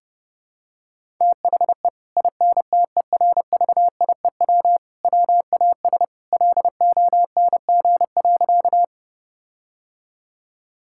Morse code, heard as a keyed tone, about 30 wpm, 700 Hz. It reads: THE INTERVIEW WAS LONG.
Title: Sentences from Top 1600 Words 30wpm